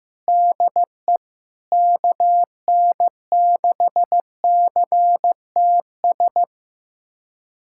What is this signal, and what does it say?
Morse code, heard as a keyed tone, about 15 wpm, 700 Hz. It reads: DE KN6CTS
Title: USA Callsign DE Callsign 15wpm